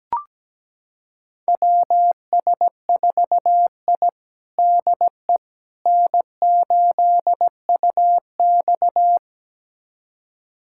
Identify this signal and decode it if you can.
Morse code, heard as a keyed tone, about 17 wpm, 700 Hz. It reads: WS4I DE N8UX
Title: USA Callsign DE Callsign - 4 Characters - 17wpm